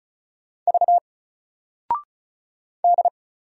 Morse code, heard as a keyed tone, about 36 wpm, 700 Hz. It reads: V D
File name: Single-Letter-Number-35wpm